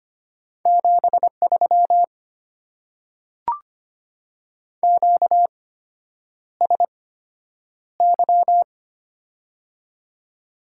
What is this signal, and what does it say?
Morse code, heard as a keyed tone, about 25 wpm, spaced as for 9 wpm, 700 Hz. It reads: 73 QSY
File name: Top 90 QSO Elements - 8x Character Spacing - 25wpm